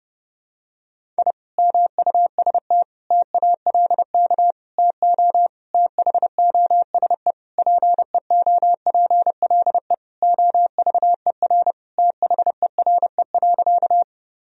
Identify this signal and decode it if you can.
Morse code, heard as a keyed tone, about 30 wpm, 700 Hz. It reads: I MUST TALK TO THOSE PEOPLE OVER THERE.
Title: Sentences from Top 1000 Words - Review Part 1 - 30wpm